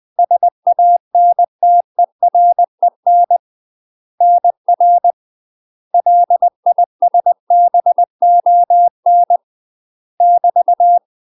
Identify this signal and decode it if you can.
Morse code, heard as a keyed tone, about 20 wpm, 700 Hz. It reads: SANTEREN NR LISBON <BT>
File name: UK MASTER QSOs TRAINING / 20WPM / UK MASTER QSOs 0070